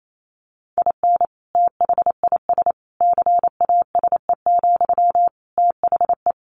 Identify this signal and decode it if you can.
Morse code, heard as a keyed tone, about 28 wpm, 700 Hz. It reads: IN THIS CASE, THE
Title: Sentences from Top 1500 Words 28wpm